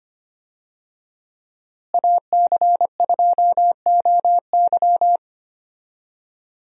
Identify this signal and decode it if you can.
Morse code, heard as a keyed tone, about 25 wpm, 700 Hz. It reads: AC2OY
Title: USA Call Signs - 5 Characters - Repeated 2x - 25wpm